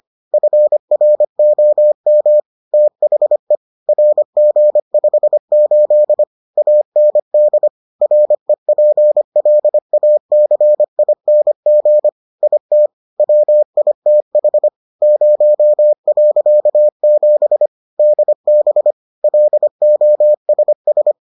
Morse code, heard as a keyed tone, about 25 wpm, 600 Hz. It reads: FROM THE RG58 AND REPLACING IT WITH 0.7 DB LOSS